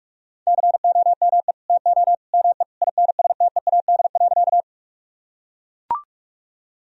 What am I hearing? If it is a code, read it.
Morse code, heard as a keyed tone, about 41 wpm, 700 Hz. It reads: COME TO ME INSTEAD.
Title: Sentences from Top 1000 Words - Review Part 4 - 45wpm